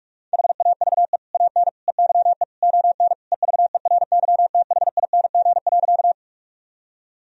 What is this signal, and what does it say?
Morse code, heard as a keyed tone, about 46 wpm, 700 Hz. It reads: HAVE AN EYE ON EVERYTHING.
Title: Sentences from Top 1000 Words - Review Part 1 - 45wpm